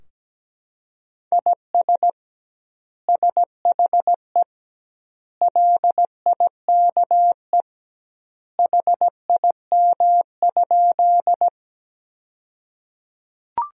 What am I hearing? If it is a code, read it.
Morse code, heard as a keyed tone, about 17 wpm, 700 Hz. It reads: IS SHE LIKE HIM?
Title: 2x Word-Spacing - Sentences from Top 100 Words - 17wpm